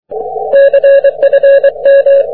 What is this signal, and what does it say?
Morse code, heard as a keyed tone, about 23 wpm, 550 Hz. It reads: CFM
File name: wp / e21ejc 40m cw